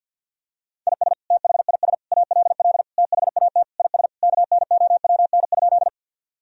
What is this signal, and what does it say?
Morse code, heard as a keyed tone, about 50 wpm, 700 Hz. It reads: IS THIS ALL THAT IS KNOWN?
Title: Sentences from Top 1000 Words - Review Part 3 - 50wpm